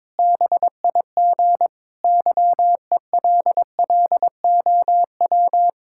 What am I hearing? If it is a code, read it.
Morse code, heard as a keyed tone, about 22 wpm, 700 Hz. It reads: BIG YELLOW
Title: Pangram Challenge v2 - 22wpm